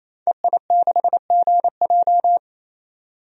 Morse code, heard as a keyed tone, about 28 wpm, 700 Hz. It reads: EI6GJ
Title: DXCC-100 Call Signs - 5 Characters - 28wpm